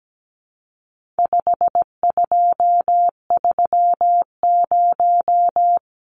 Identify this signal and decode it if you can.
Morse code, heard as a keyed tone, about 17 wpm, 700 Hz. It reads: 5230
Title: Numbers - 4-Digit 17wpm